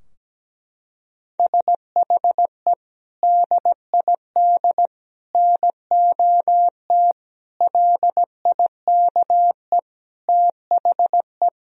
Morse code, heard as a keyed tone, about 17 wpm, 700 Hz. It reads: SHE DID NOT LIKE THE
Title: Sentences from Top 1000 Words - Review Part 3 - 17wpm